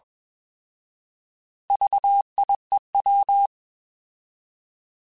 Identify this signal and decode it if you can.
Morse code, heard as a keyed tone, about 21 wpm, 800 Hz. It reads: VIEW
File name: Alphabetical / Alphabetical mp3s / Vs @20